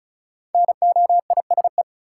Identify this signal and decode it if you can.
Morse code, heard as a keyed tone, about 34 wpm, 700 Hz. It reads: NOISE
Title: Five Letter Words - 35wpm